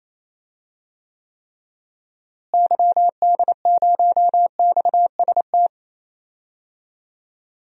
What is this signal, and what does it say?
Morse code, heard as a keyed tone, about 28 wpm, 700 Hz. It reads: YD0XST
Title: DXCC-100 Call Signs - 6 Characters - Repeated 3x - 28wpm